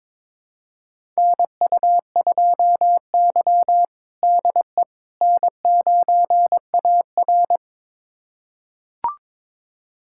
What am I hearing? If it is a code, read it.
Morse code, heard as a keyed tone, about 22 wpm, 700 Hz. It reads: NU2Y DE N9AR
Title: USA Callsign DE Callsign - 4 Characters - 22wpm